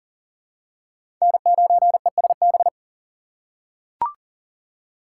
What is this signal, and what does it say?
Morse code, heard as a keyed tone, about 37 wpm, 700 Hz. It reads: N9ESB
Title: USA Call Signs - 5 Characters - 2x Character Spacing - 40wpm